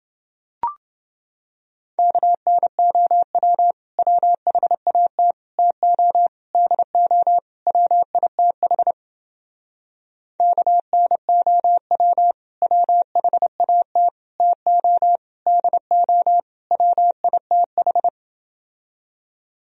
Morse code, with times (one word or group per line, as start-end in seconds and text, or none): 2.0-3.7 KNOW
4.0-5.3 WHAT
5.6-6.3 TO
6.5-7.4 DO
7.7-8.9 WITH
10.4-12.3 KNOW
12.6-14.1 WHAT
14.4-15.2 TO
15.5-16.4 DO
16.7-18.1 WITH